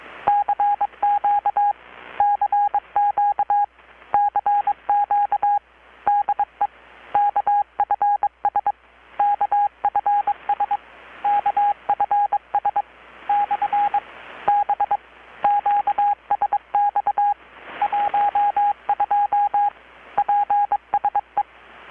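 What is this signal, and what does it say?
Morse code, heard as a keyed tone, about 22 wpm, 820 Hz, with radio noise. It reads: CQ CQ CQ DE KFS KFS KFS/B QSX 12 PSE
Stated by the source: Morse from KPH's Night of Nights XX maritime radiotelegraphy event, received in Chiapas, Mexico about 3,800 km away on a Kenwood R-600 with a grounded 10 m long-wire antenna.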